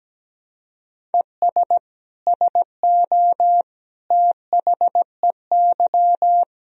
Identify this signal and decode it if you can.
Morse code, heard as a keyed tone, about 17 wpm, 700 Hz. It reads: ES SO THEY
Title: Sentences from Top 400 Words and Abbreviations 17wpm